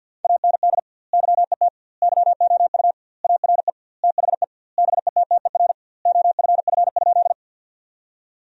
Morse code, heard as a keyed tone, about 52 wpm, 700 Hz. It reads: AND YET YOU ARE THE BETTER OFF?
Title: 2x Word-Spacing - Sentences from Top 500 Words - 50wpm